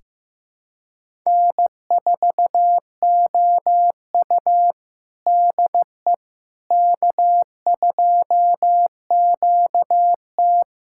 Morse code, heard as a keyed tone, about 15 wpm, 700 Hz. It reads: N4OU DE K2QT
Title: USA Callsign DE Callsign - 4 Characters - 15wpm